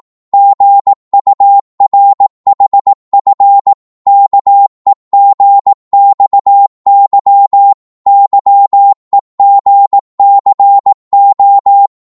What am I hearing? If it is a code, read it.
Morse code, heard as a keyed tone, about 18 wpm, 800 Hz. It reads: GURHF KEGXY YEGCO
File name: Morse / ltr-18wpm